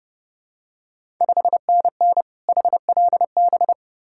Morse code, heard as a keyed tone, about 30 wpm, 700 Hz. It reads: 5NN HLB
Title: Ontario QSO Party Exchange 30wpm